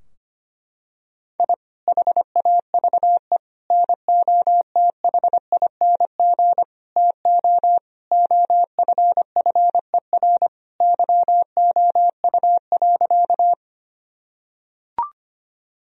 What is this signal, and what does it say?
Morse code, heard as a keyed tone, about 25 wpm, 700 Hz. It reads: I HAVE NOTHING TO OFFER YOU.
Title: Sentences from Top 1000 Words - Review Part 4 - 25wpm